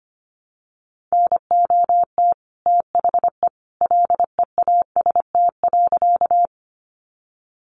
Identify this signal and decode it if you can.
Morse code, heard as a keyed tone, about 25 wpm, 700 Hz. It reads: NOT THE LEAST.
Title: Sentences from Top 700 Words 25wpm